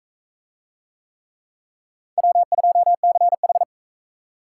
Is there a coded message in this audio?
Yes: Morse code from 2.2 to 3.6 s, heard as a keyed tone, about 40 wpm, 700 Hz. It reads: W2CH